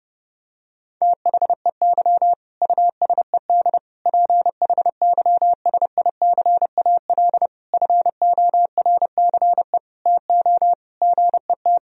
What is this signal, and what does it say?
Morse code, heard as a keyed tone, about 30 wpm, 700 Hz. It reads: THEY USED PHYSICAL FORCE TO GET
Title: Sentences from Top 1000 Words - Review Part 4 - 30wpm